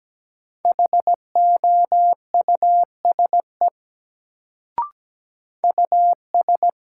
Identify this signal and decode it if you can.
Morse code, heard as a keyed tone, about 17 wpm, 700 Hz. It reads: HOUSE US
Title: Sets of 2 Words - New Word Drill - 17wpm